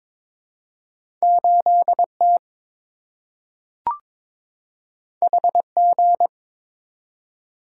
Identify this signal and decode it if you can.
Morse code, heard as a keyed tone, about 22 wpm, 700 Hz. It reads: 8T HG
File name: practice / 22wpm / Two Letter-Number Combinations - 22wpm